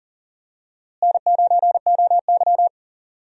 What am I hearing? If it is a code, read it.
Morse code, heard as a keyed tone, about 40 wpm, 700 Hz. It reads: N9OY
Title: USA Call Signs - 4 Characters - Repeated 2x - 40wpm